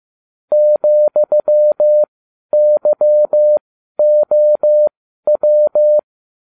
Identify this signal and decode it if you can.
Morse code, heard as a keyed tone, about 15 wpm, spaced as for 11 wpm, 600 Hz. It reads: ,YOW